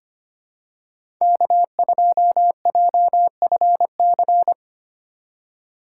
Morse code, heard as a keyed tone, about 25 wpm, 700 Hz. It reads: K2JFC
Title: USA Call Signs - 5 Characters - 3x Character Spacing - 25wpm